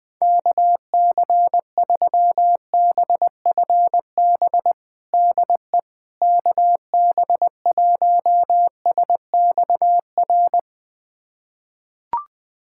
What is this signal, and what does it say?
Morse code, heard as a keyed tone, about 20 wpm, 700 Hz. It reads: KC3BFB DE KB1SXR
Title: USA Callsign DE Callsign - 6 Characters - 20wpm